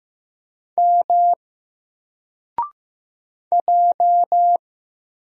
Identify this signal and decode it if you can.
Morse code, heard as a keyed tone, about 15 wpm, 700 Hz. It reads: M J
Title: Single Letter-Number 15wpm